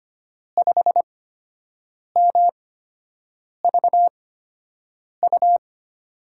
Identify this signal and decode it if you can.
Morse code, heard as a keyed tone, about 25 wpm, spaced as for 7 wpm, 700 Hz. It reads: 5MVU
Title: USA Call Signs - 6 Characters - 8x Character Spacing - 25wpm